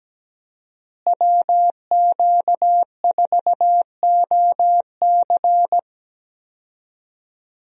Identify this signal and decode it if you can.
Morse code, heard as a keyed tone, about 17 wpm, 700 Hz. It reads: WQ4OC
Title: USA Call Signs - 5 Characters - Repeated 3x - 17wpm